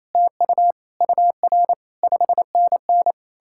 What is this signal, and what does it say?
Morse code, heard as a keyed tone, about 38 wpm, spaced as for 34 wpm, 700 Hz. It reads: TU UR 5NN
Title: Speed-Racing Common QSO Phrases Encore - 28wpm